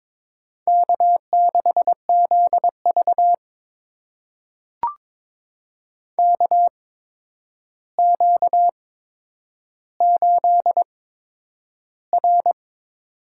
Morse code, heard as a keyed tone, about 22 wpm, spaced as for 9 wpm, 700 Hz. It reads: K6ZV KQ8R
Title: USA Call Signs - 4 Characters - 8x Character Spacing - 22wpm